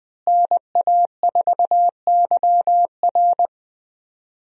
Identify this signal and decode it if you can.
Morse code, heard as a keyed tone, about 20 wpm, 700 Hz. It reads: NA4YR
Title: USA Call Signs - 5 Characters - Repeated 3x - 20wpm